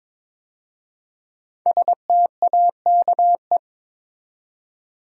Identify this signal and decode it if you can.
Morse code, heard as a keyed tone, about 22 wpm, 700 Hz. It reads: STAKE
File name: 5 Letter Words - First 3 Letters Given - 22wpm